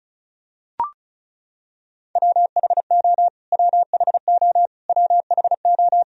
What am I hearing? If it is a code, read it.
Morse code, heard as a keyed tone, about 36 wpm, 700 Hz. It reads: WHO WHO WHO
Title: Top 100 Words Repeated 35wpm